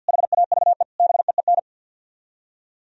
Morse code, heard as a keyed tone, about 64 wpm, 700 Hz. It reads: HAVE BEEN